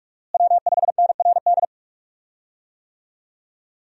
Morse code, heard as a keyed tone, about 42 wpm, 700 Hz. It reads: W5NRD
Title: USA Call Signs - 5 Characters - Repeated 2x - 45wpm